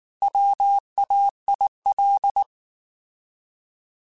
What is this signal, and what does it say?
Morse code, heard as a keyed tone, about 19 wpm, 800 Hz. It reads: WAIL
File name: Alphabetical / Alphabetical mp3s / Ws @18